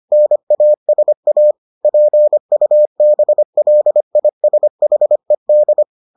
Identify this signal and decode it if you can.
Morse code, heard as a keyed tone, about 25 wpm, 600 Hz. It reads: NASA PUBLISHED